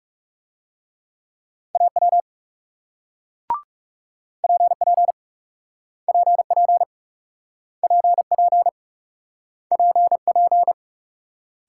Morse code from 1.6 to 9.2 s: AW PP PP PP

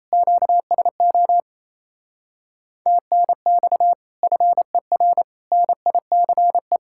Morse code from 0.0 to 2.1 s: QSO

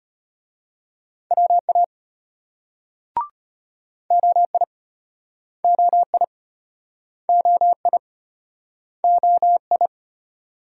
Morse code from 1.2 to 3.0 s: WA

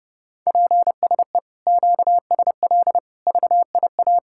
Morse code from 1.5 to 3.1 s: QSL

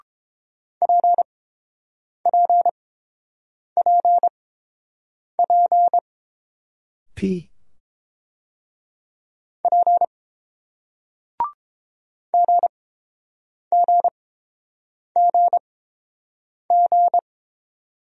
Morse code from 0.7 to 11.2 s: PPPP P

